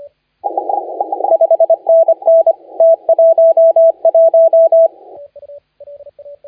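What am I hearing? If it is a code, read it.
Morse code, heard as a keyed tone, about 25 wpm, 660 Hz, with radio noise. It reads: 5NN T11
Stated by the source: Morse contest audio recorded by LA4C during the SAC CW 2018 contest.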